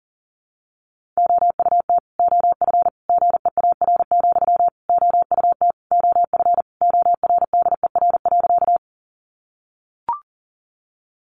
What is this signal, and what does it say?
Morse code, heard as a keyed tone, about 37 wpm, 700 Hz. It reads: OUT OF GEAR, OUT OF ORDER.
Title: Sentences from Top 1400 Words 40wpm